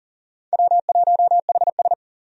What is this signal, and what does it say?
Morse code, heard as a keyed tone, about 40 wpm, 700 Hz. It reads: W1HS
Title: USA Call Signs - 4 Characters - 40wpm